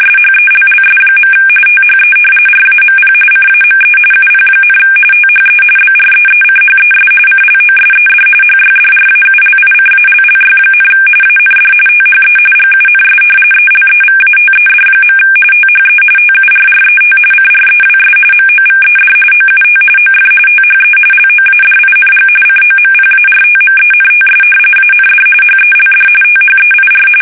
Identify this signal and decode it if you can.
RTTY, about 45.45 baud. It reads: SHELLS. THE RAMP CAME UP, THE BREECHES CLOSED HYDRAULICALLY, AND THE GUNS ELEVATED. THE TURRET CREWS MOVED OUT OF THE LOADING COMPARTMENTS AND HELD THEIR
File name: fsk850 / battleshipnj